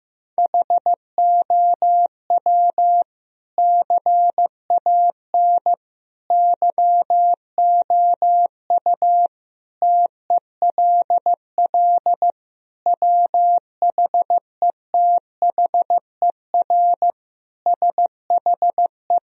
Morse code, heard as a keyed tone, about 15 wpm, 700 Hz. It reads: HOW CAN YOU TELL WHETHER SHE
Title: Sentences from Top 1000 Words - Review Part 3 - 15wpm